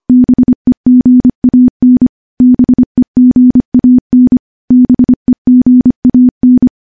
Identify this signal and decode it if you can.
Morse code, heard as a keyed tone, about 25 wpm, 250 Hz. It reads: BEGAN BEGAN BEGAN
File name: morse / morse-training-25WPM